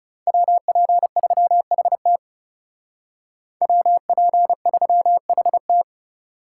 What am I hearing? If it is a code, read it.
Morse code, heard as a keyed tone, about 35 wpm, 700 Hz. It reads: WP3HT WP3HT
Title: Speed Racing - USA 2x2 Call Signs 30wpm